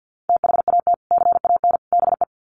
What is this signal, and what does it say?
Morse code, heard as a keyed tone, about 50 wpm, 700 Hz. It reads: THAT CAN BE
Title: Sets of 3 Words 50wpm